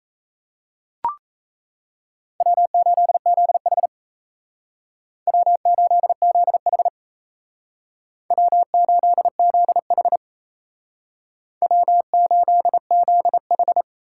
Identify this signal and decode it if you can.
Morse code, heard as a keyed tone, about 35 wpm, 700 Hz. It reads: W8ZH W8ZH W8ZH W8ZH
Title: Speed Racing - USA 1x2 Call Signs 28wpm